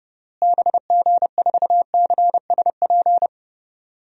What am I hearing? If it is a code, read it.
Morse code, heard as a keyed tone, about 30 wpm, 700 Hz. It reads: BG4CSP
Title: DXCC-100 Call Signs - 6 Characters - Repeated 3x - 30wpm